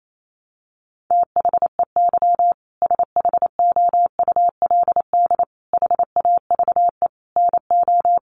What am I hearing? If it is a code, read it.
Morse code, heard as a keyed tone, about 28 wpm, 700 Hz. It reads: THEY SHOULD HAVE NO
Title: Sentences from Top 1500 Words 28wpm